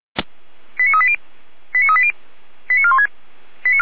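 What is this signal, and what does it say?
Five-tone selective call (ZVEI), heard as quick runs of five tones, tones of about 70 ms. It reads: 89380 89380 89416
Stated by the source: ZVEI five-tone call received with an SDR.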